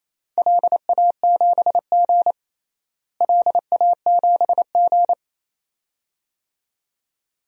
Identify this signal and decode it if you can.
Morse code, heard as a keyed tone, about 28 wpm, 700 Hz. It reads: LA7G LA7G
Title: DXCC-100 Call Signs - 4 Characters - Repeated 2x - 28wpm